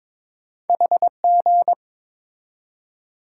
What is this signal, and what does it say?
Morse code, heard as a keyed tone, about 22 wpm, 690 Hz. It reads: HG
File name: practice / 22wpm / Two Letter-Number Combinations - 22wpm